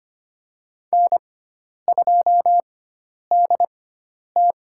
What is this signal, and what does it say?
Morse code, heard as a keyed tone, about 25 wpm, spaced as for 10 wpm, 700 Hz. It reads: N2DT
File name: Lesson 16g - Letter c - Call Signs 25wpm